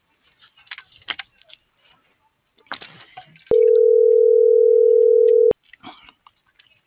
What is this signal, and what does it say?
Ringback tone (North America), 440 + 480 Hz.